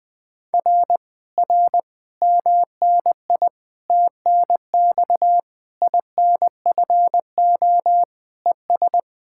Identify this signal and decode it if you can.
Morse code, heard as a keyed tone, about 20 wpm, 700 Hz. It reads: R R MNI TNX INFO ES
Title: Speed-Racing Common QSO Phrases - 20wpm